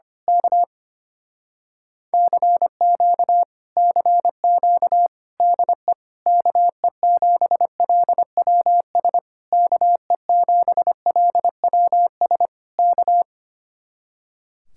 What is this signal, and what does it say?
Morse code, heard as a keyed tone, about 24 wpm, 700 Hz. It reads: K CQ CQ DE KE7LWS KE7LWS K
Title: Speed-Racing Common QSO Phrases - 25wpm